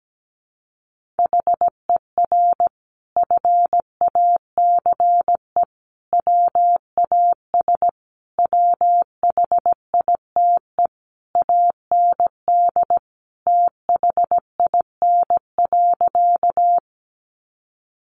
Morse code, heard as a keyed tone, about 17 wpm, 700 Hz. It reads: HER FACE WAS WHITE AND THIN.
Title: Sentences from Top 1200 Words 17wpm